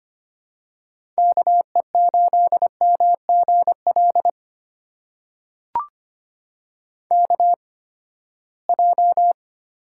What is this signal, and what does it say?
Morse code, heard as a keyed tone, about 25 wpm, spaced as for 17 wpm, 700 Hz. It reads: KE8MGL KJ
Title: USA Call Signs - 6 Characters - 8x Character Spacing - 25wpm